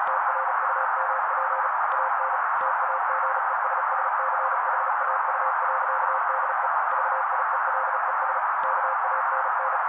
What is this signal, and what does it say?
Morse code, heard as a keyed tone, about 37 wpm, 530 Hz. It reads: TRUMP AN TAGESSCHAU.DE VIELES KANN
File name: daily / 35wpm noise